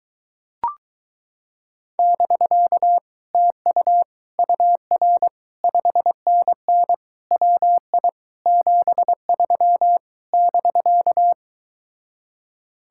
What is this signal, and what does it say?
Morse code, heard as a keyed tone, about 24 wpm, 700 Hz. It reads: <BK> TU UR 5NN WI 73 <BK>